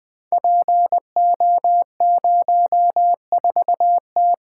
Morse code, heard as a keyed tone, about 20 wpm, 700 Hz. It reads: PO04T
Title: Code Groups - 5 Letter:Numbers - 20wpm